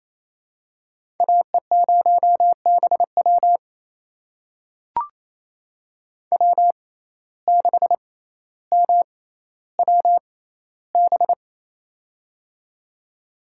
Morse code, heard as a keyed tone, about 28 wpm, spaced as for 14 wpm, 700 Hz. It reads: AE0BW W6MWB
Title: USA Call Signs - 5 Characters - 6x Character Spacing - 28wpm